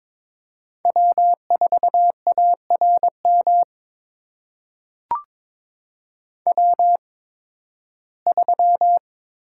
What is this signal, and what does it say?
Morse code, heard as a keyed tone, about 22 wpm, spaced as for 14 wpm, 700 Hz. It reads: W4ARM W3
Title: USA Call Signs - 5 Characters - 8x Character Spacing - 22wpm